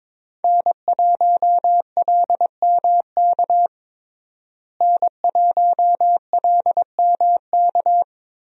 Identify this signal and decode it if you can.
Morse code, heard as a keyed tone, about 22 wpm, 700 Hz. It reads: N1LMK N1LMK
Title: USA Call Signs - 5 Characters - Repeated 2x - 22wpm